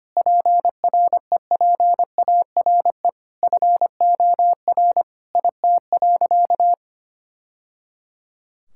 Morse code, heard as a keyed tone, about 25 wpm, 700 Hz. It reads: PREPARE FOR IT.
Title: Sentences from Top 1000 Words - Review Part 6 - 25wpm